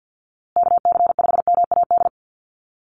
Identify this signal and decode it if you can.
Morse code, heard as a keyed tone, about 50 wpm, 700 Hz. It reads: KC5MAD
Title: USA Call Signs Repeated 3x 50wpm